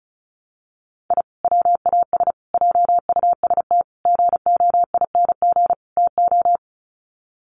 Morse code, heard as a keyed tone, about 34 wpm, 700 Hz. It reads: I WAS JUST GOING TO
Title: Sets of 5 Words Encore 35wpm